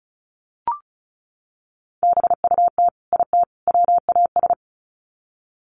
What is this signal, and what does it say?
Morse code, heard as a keyed tone, about 34 wpm, 700 Hz. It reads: BUT IT WAS
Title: Sets of 3 Words 35wpm